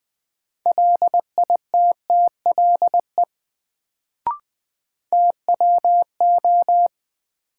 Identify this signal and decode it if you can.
Morse code, heard as a keyed tone, about 20 wpm, 700 Hz. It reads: LITTLE TWO
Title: Sets of 2 Words - New Word Drill - 20wpm